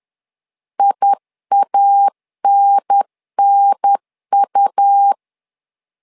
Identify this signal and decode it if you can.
Morse code, heard as a keyed tone, about 10 wpm, 800 Hz. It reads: IANNU
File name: LEZIONI / 2 UHDN / 2 UHDN 10wpm 1904